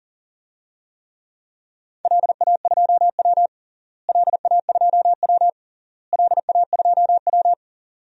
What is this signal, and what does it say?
Morse code, heard as a keyed tone, about 37 wpm, 700 Hz. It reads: LA2W LA2W LA2W
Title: DXCC-100 Call Signs - 4 Characters - Repeated 3x - 40wpm